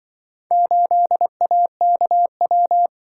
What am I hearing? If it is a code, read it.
Morse code, heard as a keyed tone, about 24 wpm, 700 Hz. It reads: 8AKW
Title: Speed-Racing - Four Letter-Number Combinations - 20wpm